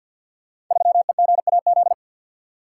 Morse code, heard as a keyed tone, about 50 wpm, 700 Hz. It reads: 3EGAZ
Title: Code Groups - 5 Letter:Numbers - 50wpm